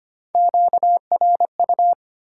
Morse code, heard as a keyed tone, about 25 wpm, 700 Hz. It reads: QRU
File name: Top 90 QSO Elements - 4x Character Spacing - 25wpm